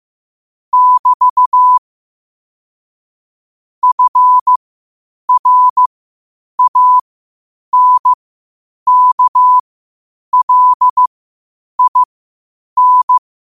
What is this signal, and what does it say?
Morse code, heard as a keyed tone, about 15 wpm, spaced as for 8 wpm, 1000 Hz. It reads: <BT> FRANKLIN
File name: 8wpm